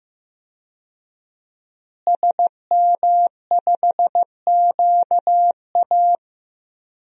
Morse code, heard as a keyed tone, about 15 wpm, 700 Hz. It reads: SM5QA
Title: DXCC-100 Call Signs - 5 Characters - Repeated 2x - 15wpm